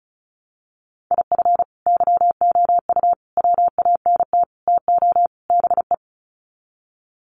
Morse code, heard as a keyed tone, about 34 wpm, 700 Hz. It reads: IF YOU WANT TO BE
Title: Sets of 5 Words Encore 35wpm